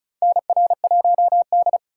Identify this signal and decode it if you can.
Morse code, heard as a keyed tone, about 34 wpm, 700 Hz. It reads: NR1D
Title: USA Call Signs - 4 Characters - 2x Character Spacing - 35wpm